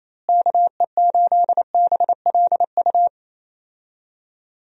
Morse code, heard as a keyed tone, about 27 wpm, 700 Hz. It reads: KE8BLU
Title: USA Call Signs - 6 Characters - Repeated 3x - 28wpm